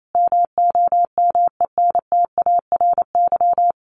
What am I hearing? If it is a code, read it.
Morse code, heard as a keyed tone, about 28 wpm, 700 Hz. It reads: MOMENTARY
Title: A Sound Of Thunder 28wpm